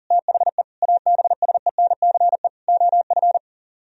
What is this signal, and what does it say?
Morse code, heard as a keyed tone, about 53 wpm, 700 Hz. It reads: THE ABSENCE OF